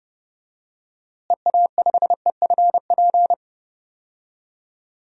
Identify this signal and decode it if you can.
Morse code, heard as a keyed tone, about 30 wpm, 700 Hz. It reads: EA5EFP
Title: DXCC-100 Call Signs - 6 Characters - Repeated 2x - 30wpm